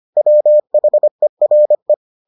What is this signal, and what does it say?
Morse code, heard as a keyed tone, about 25 wpm, 600 Hz. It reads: WHERE